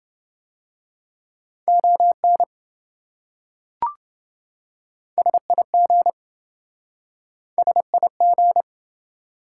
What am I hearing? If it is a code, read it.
Morse code, heard as a keyed tone, about 27 wpm, 700 Hz. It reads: ON SIG SIG